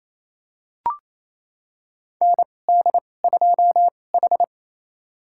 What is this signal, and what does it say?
Morse code, heard as a keyed tone, about 28 wpm, spaced as for 20 wpm, 700 Hz. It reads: ND2H
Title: USA Call Signs - 4 Characters - 2x Character Spacing - 28wpm